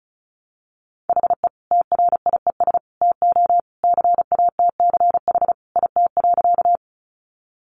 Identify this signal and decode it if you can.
Morse code, heard as a keyed tone, about 34 wpm, 700 Hz. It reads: HE TRIES TO CATCH IT.
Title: Additional Easy Phrases 35wpm